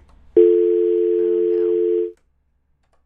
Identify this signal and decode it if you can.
Dial tone (North America), 350 + 440 Hz.
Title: on Between The Lines on WFMU